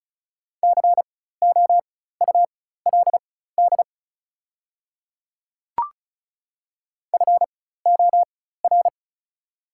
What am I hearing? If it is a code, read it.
Morse code, heard as a keyed tone, about 36 wpm, spaced as for 16 wpm, 690 Hz. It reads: COULD FOR
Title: 4x Character Spacing - Top 100 Words - 35wpm